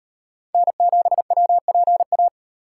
Morse code, heard as a keyed tone, about 38 wpm, 700 Hz. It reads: N7WPA